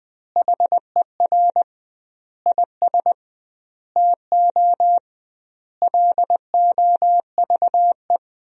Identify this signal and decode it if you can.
Morse code, heard as a keyed tone, about 20 wpm, 700 Hz. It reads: HER IS TO LOVE
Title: 2x Word-Spacing - Sentences from Top 400 Words - 20wpm